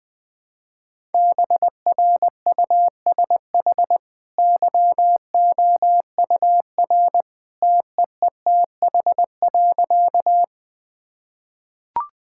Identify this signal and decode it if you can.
Morse code, heard as a keyed tone, about 19 wpm, 700 Hz. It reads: BRUSH YOUR TEETH.
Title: Easy Sentences with Hints - 20wpm